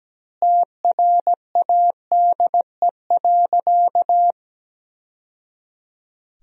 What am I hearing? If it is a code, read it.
Morse code, heard as a keyed tone, about 17 wpm, 700 Hz. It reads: TRADE.
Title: Sentences from Top 1000 Words - Review Part 4 - 17wpm